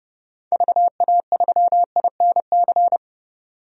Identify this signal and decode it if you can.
Morse code, heard as a keyed tone, about 31 wpm, 690 Hz. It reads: VA3INC
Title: DXCC-100 Call Signs - 6 Characters - 30wpm